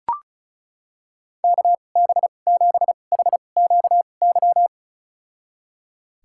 Morse code, heard as a keyed tone, about 34 wpm, spaced as for 25 wpm, 700 Hz. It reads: KB7HQY
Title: USA Call Signs - 2x Character Spacing - 35wpm